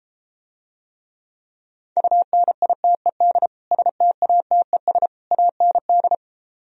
Morse code, heard as a keyed tone, about 34 wpm, 700 Hz. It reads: UNITED STATES AND